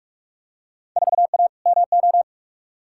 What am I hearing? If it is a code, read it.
Morse code, heard as a keyed tone, about 47 wpm, 700 Hz. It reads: 4A MO
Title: Field Day Exchanges - 45wpm